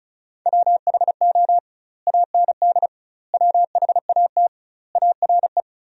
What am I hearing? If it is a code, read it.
Morse code, heard as a keyed tone, about 34 wpm, 700 Hz. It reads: WHO AND WHAT ARE
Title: 2x Word-Spacing - Sentences from Top 100 Words - 35wpm